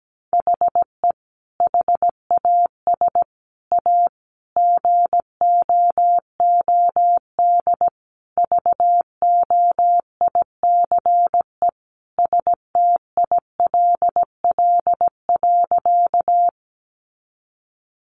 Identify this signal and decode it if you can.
Morse code, heard as a keyed tone, about 17 wpm, 700 Hz. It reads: HE HAS A GOOD VOICE STILL.
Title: Sentences from Top 500 Words 17wpm